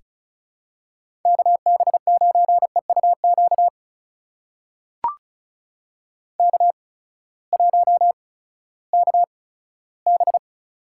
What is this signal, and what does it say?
Morse code, heard as a keyed tone, about 36 wpm, spaced as for 18 wpm, 700 Hz. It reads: KB9EUQ K1KB
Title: USA Call Signs - 8x Character Spacing - 35wpm